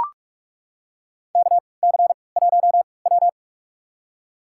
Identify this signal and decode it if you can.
Morse code, heard as a keyed tone, about 42 wpm, spaced as for 25 wpm, 700 Hz. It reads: KC1W
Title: USA Call Signs - 4 Characters - 3x Character Spacing - 45wpm